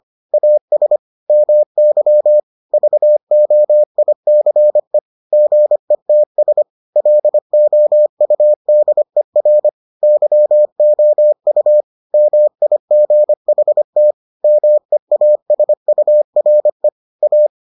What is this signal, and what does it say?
Morse code, heard as a keyed tone, about 25 wpm, 600 Hz. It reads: AS MY VOICE GETS LOUDER YOU MIGHT MEASURE A